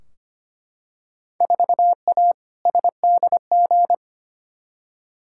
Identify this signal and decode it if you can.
Morse code, heard as a keyed tone, about 25 wpm, 700 Hz. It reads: 4A SDG
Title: Field Day Exchanges - 25wpm